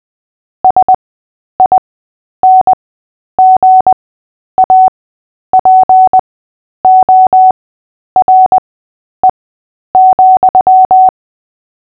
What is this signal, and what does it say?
Morse code, heard as a keyed tone, about 20 wpm, spaced as for 10 wpm, 750 Hz. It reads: SINGAPORE,
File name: cw / fists / Short-Story-3-10wpm